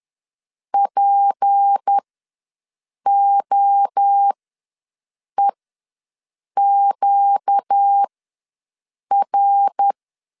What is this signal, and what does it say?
Morse code, heard as a keyed tone, about 10 wpm, spaced as for 6 wpm, 800 Hz. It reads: POEQR
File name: LEZIONI / 7 GF9Z / 7 GF ws 10wpm 3918